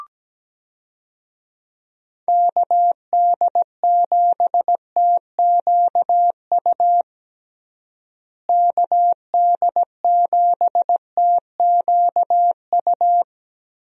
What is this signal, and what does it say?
Morse code, heard as a keyed tone, about 17 wpm, 700 Hz. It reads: KD7TQU KD7TQU
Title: USA Call Signs - 6 Characters - Repeated 3x - 17wpm